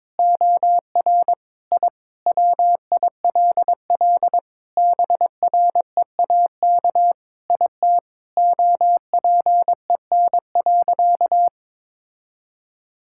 Morse code, heard as a keyed tone, about 22 wpm, 700 Hz. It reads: OR I WILL BREAK IT OPEN.
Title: Sentences from Top 1000 Words - Review Part 6 - 22wpm